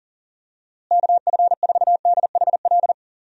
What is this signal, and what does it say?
Morse code, heard as a keyed tone, about 39 wpm, 700 Hz. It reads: KF4DHL
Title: USA Call Signs - 6 Characters - Repeated 2x - 40wpm